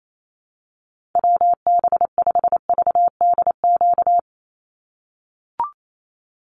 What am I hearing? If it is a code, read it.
Morse code, heard as a keyed tone, about 28 wpm, 700 Hz. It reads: WB5VDQ
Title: USA Call Signs 28wpm